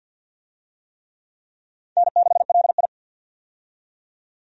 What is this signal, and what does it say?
Morse code, heard as a keyed tone, about 45 wpm, 700 Hz. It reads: N6LI
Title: USA Call Signs - 4 Characters - 50wpm